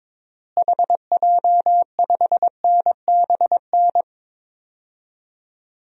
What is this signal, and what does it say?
Morse code, heard as a keyed tone, about 22 wpm, 700 Hz. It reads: HJ5NBN
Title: DXCC-100 Call Signs - 6 Characters - Repeated 2x - 22wpm